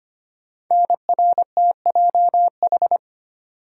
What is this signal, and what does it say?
Morse code, heard as a keyed tone, about 25 wpm, 700 Hz. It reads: NRTJH
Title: Code Groups - 5 Letters - 25wpm